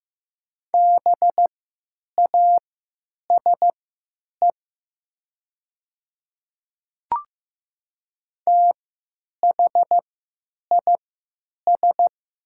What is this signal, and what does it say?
Morse code, heard as a keyed tone, about 15 wpm, spaced as for 9 wpm, 700 Hz. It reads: BASE THIS
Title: 3x Character Spacing - Top 300 Words - 15wpm